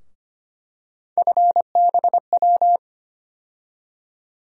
Morse code, heard as a keyed tone, about 25 wpm, 700 Hz. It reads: FBW
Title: USA Call Sign Postfixes - 3 Letters 25wpm